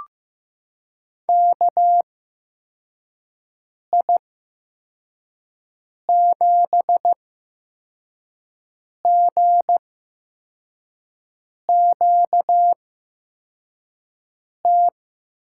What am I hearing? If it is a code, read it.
Morse code, heard as a keyed tone, about 15 wpm, spaced as for 4 wpm, 700 Hz. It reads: KI7GQT
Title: USA Call Signs - 6 Characters - 8x Character Spacing - 15wpm